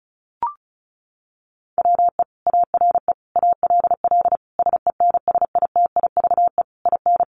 Morse code, heard as a keyed tone, about 34 wpm, 700 Hz. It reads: WE ARE ALL SENSITIVE IN
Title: Sentences from Top 1400 Words 35wpm